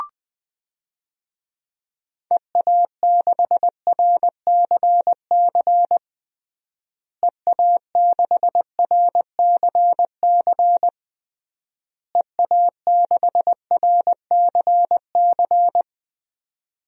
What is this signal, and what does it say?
Morse code, heard as a keyed tone, about 20 wpm, 700 Hz. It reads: EA6RCC EA6RCC EA6RCC
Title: DXCC-100 Call Signs - 6 Characters - Repeated 3x - 20wpm